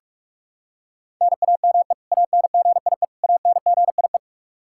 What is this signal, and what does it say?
Morse code, heard as a keyed tone, about 46 wpm, 700 Hz. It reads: NAME ANGIE ANGIE
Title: Practice QSO - Exchange 1 - Beginning - 45wpm